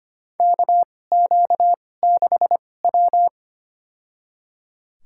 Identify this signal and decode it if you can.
Morse code, heard as a keyed tone, about 25 wpm, spaced as for 18 wpm, 700 Hz. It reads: KQ6W
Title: USA Call Signs - 4 Characters - 2x Character Spacing - 25wpm